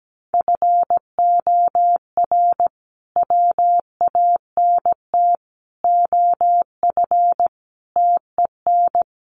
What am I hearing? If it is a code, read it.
Morse code, heard as a keyed tone, about 17 wpm, 700 Hz. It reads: FOR WANT OF TEN
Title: A Sound Of Thunder 17wpm